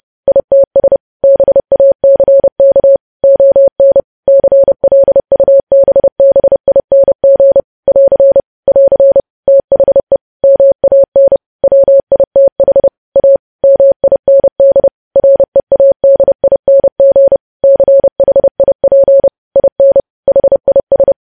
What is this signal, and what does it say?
Morse code, heard as a keyed tone, about 30 wpm, 550 Hz. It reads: ITS BACK ON CLUBBING <AR> <AR> THE MAN WITH A MIND READING CHIP IN HIS
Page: news headlines in morse code at 30 wpm